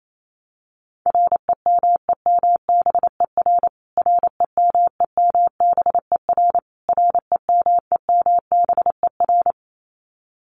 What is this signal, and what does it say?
Morse code, heard as a keyed tone, about 28 wpm, 700 Hz. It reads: REMEMBER REMEMBER REMEMBER
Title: Top 500 Words Repeated 28wpm